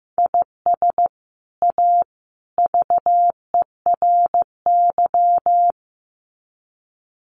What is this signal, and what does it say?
Morse code, heard as a keyed tone, about 15 wpm, 700 Hz. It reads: IS A VERY
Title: Sets of 3 Words 15wpm